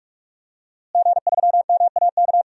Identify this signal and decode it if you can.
Morse code, heard as a keyed tone, about 41 wpm, 700 Hz. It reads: G3MAK
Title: DXCC-100 Call Signs - 5 Characters - Repeated 2x - 45wpm